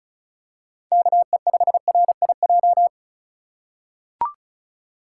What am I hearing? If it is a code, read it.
Morse code, heard as a keyed tone, about 34 wpm, 700 Hz. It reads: KE5RIJ
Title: USA Call Signs - 6 Characters - 2x Character Spacing - 35wpm